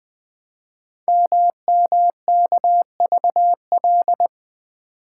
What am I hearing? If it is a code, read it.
Morse code, heard as a keyed tone, about 20 wpm, 700 Hz. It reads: MMKVL
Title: Code Groups - 5 Letters - 20wpm